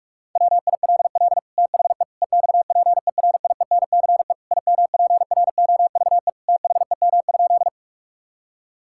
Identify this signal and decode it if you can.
Morse code, heard as a keyed tone, about 46 wpm, 700 Hz. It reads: WILL THE EXPERIENCE IMPROVE THEM?
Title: Sentences from Top 1000 Words - Review Part 5 - 45wpm